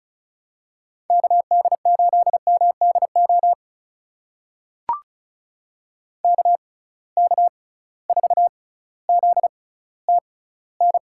Morse code, heard as a keyed tone, about 34 wpm, spaced as for 18 wpm, 700 Hz. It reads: KD8MDO KK4ZTN
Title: USA Call Signs - 6 Characters - 6x Character Spacing - 35wpm